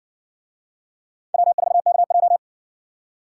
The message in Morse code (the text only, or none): W4KJ